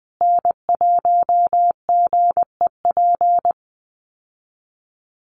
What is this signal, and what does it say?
Morse code, heard as a keyed tone, about 20 wpm, 700 Hz. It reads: N1GEP
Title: USA Call Signs Repeated 3x 20wpm